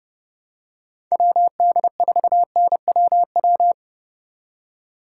Morse code, heard as a keyed tone, about 30 wpm, 700 Hz. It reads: WD4NWW
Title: USA Call Signs - 6 Characters - 3x Character Spacing - 30wpm